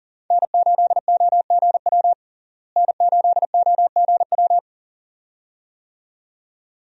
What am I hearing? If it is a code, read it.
Morse code, heard as a keyed tone, about 40 wpm, 700 Hz. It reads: N8OGW N8OGW
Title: USA Call Signs - 5 Characters - Repeated 2x - 40wpm